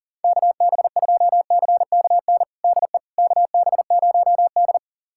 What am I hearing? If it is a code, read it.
Morse code, heard as a keyed tone, about 40 wpm, 700 Hz. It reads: KB2CKN DE KB0D
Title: USA Callsign DE Callsign 40wpm